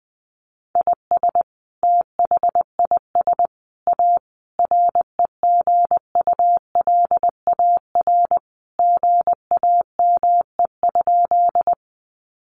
Morse code, heard as a keyed tone, about 20 wpm, 700 Hz. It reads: IS THIS A REGULAR GAME?
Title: Sentences from Top 900 Words 20wpm